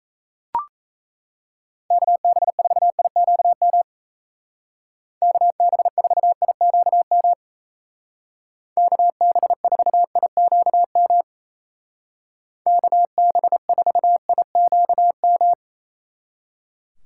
Morse code, heard as a keyed tone, about 35 wpm, 700 Hz. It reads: KB4IQM KB4IQM KB4IQM KB4IQM